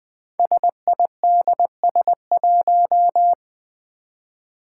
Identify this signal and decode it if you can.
Morse code, heard as a keyed tone, about 20 wpm, 700 Hz. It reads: SIDS1
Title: Five Challenging Letter-Number Combinations 20wpm